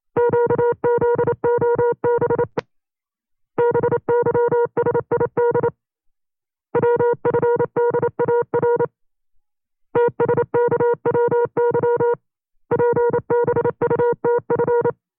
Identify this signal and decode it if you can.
Morse code, heard as a keyed tone, about 29 wpm, 480 Hz. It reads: QZOB BYSID WFDAR TSKWY PBUTF
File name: CW-practice/25WPM